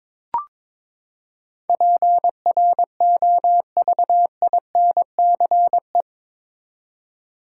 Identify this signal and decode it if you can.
Morse code, heard as a keyed tone, about 22 wpm, 700 Hz. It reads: PROVINCE
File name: practice / 22wpm / Eight Letter Words - 22wpm